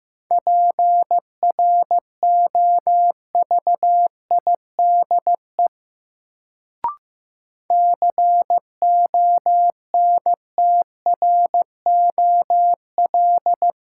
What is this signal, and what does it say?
Morse code, heard as a keyed tone, about 15 wpm, 700 Hz. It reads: PROVIDE CONTROL
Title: Sentences from Top 1000 Words - Review Part 2 - New Word Drill - 15wpm